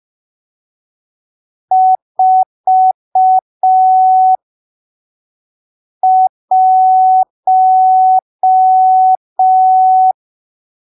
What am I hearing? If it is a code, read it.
Morse code, heard as a keyed tone, about 5 wpm, spaced as for 3 wpm, 750 Hz. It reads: 41
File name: arquivos / texto / texto 12 05ppm